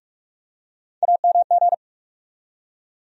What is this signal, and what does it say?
Morse code, heard as a keyed tone, about 45 wpm, 700 Hz. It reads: AMG
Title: USA Call Sign Postfixes - 3 Letters 45wpm